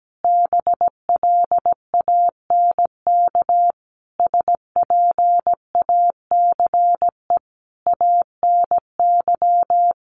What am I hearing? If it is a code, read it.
Morse code, heard as a keyed tone, about 17 wpm, 700 Hz. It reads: BLANK SPACE ANY
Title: Sentences from Top 1500 Words 17wpm